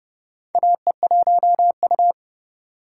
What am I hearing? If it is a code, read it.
Morse code, heard as a keyed tone, about 30 wpm, 700 Hz. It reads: AE1U
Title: USA Call Signs - 3x Character Spacing - 30wpm